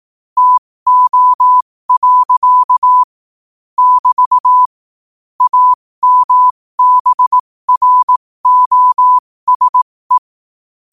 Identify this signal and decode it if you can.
Morse code, heard as a keyed tone, about 18 wpm, spaced as for 15 wpm, 1000 Hz. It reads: TO. <BT> AMBROSE